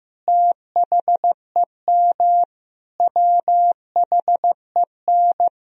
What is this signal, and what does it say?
Morse code, heard as a keyed tone, about 15 wpm, 700 Hz. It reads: THEM WHEN
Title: Sentences from Top 1000 Words - Review Part 2 - 15wpm